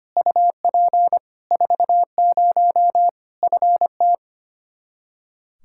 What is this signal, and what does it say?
Morse code, heard as a keyed tone, about 25 wpm, 700 Hz. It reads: UP 40 FT
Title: Practice QSO - Exchange 2 - Info - 25wpm